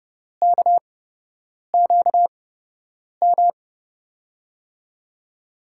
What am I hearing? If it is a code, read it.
Morse code, heard as a keyed tone, about 30 wpm, spaced as for 8 wpm, 700 Hz. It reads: KQM
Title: USA Call Signs - 6 Characters - 8x Character Spacing - 30wpm